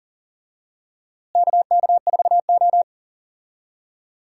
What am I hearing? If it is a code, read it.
Morse code, heard as a keyed tone, about 38 wpm, 700 Hz. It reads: KK4O
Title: USA Call Signs - 4 Characters - Repeated 2x - 40wpm